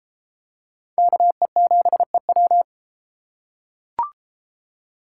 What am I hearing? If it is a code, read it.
Morse code, heard as a keyed tone, about 29 wpm, 700 Hz. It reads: KE7EW